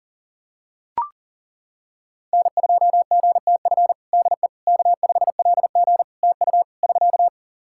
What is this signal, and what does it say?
Morse code, heard as a keyed tone, about 40 wpm, 700 Hz. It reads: N2GTF DE K5LG TU <SK>
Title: Practice QSO - Exchange 4 - Ending - 40wpm